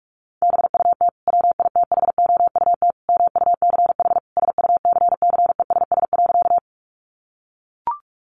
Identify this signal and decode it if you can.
Morse code, heard as a keyed tone, about 44 wpm, 700 Hz. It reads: BUT WITHOUT MUCH SUCCESS.
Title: Sentences from Top 900 Words 45wpm